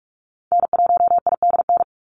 Morse code, heard as a keyed tone, about 44 wpm, 700 Hz. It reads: N1IDN
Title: NA SOTA Activator Callsigns 45wpm